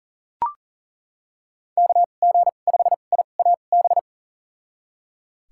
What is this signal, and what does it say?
Morse code, heard as a keyed tone, about 40 wpm, spaced as for 29 wpm, 700 Hz. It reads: KG5IAB
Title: USA Call Signs - 6 Characters - 2x Character Spacing - 40wpm